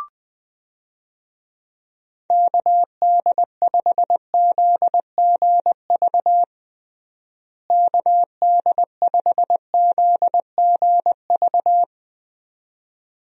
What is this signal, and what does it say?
Morse code, heard as a keyed tone, about 20 wpm, 700 Hz. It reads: KD5ZGV KD5ZGV
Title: USA Call Signs - 6 Characters - Repeated 2x - 20wpm